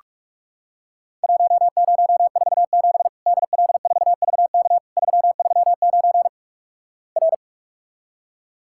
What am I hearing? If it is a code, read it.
Morse code, heard as a keyed tone, about 48 wpm, 700 Hz. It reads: 1047 DL4VK 339 R
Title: Worked All Europe DX Contest - QTC Exchange - 45wpm